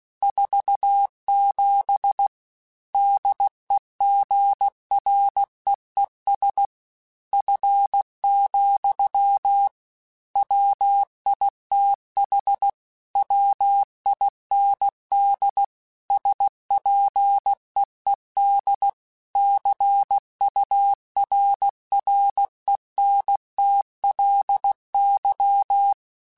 Morse code, heard as a keyed tone, about 16 wpm, 800 Hz. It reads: 47 DEGREES F, WITH WIND SPEED CURRENTLY